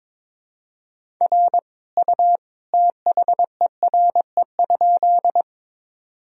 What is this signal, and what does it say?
Morse code, heard as a keyed tone, about 22 wpm, 700 Hz. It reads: R U THERE?
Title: Sentences from Top 100 Words and Abbreviations 22wpm